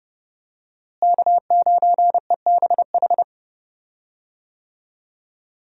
Morse code, heard as a keyed tone, about 29 wpm, 700 Hz. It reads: K9EBH
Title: USA Call Signs - 5 Characters - Repeated 2x - 30wpm